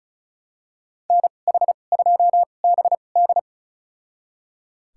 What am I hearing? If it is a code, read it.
Morse code, heard as a keyed tone, about 36 wpm, spaced as for 25 wpm, 700 Hz. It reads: NH2BD
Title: USA Call Signs - 5 Characters - 2x Character Spacing - 35wpm